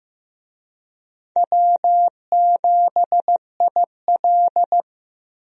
Morse code, heard as a keyed tone, about 16 wpm, 700 Hz. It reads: W7IL
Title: USA Call Signs - 4 Characters - Repeated 2x - 15wpm